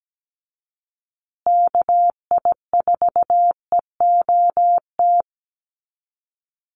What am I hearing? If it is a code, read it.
Morse code, heard as a keyed tone, about 17 wpm, 700 Hz. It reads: KI4EOT
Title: USA Call Signs Repeated 3x 17wpm